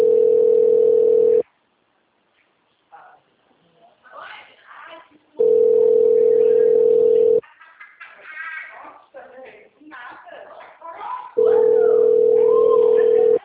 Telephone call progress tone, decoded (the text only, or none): ringback tone